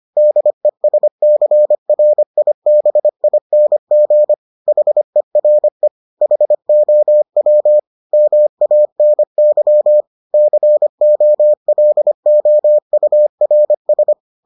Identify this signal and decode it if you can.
Morse code, heard as a keyed tone, about 25 wpm, 600 Hz. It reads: DESCRIBING HERE HOW MANY COLOURS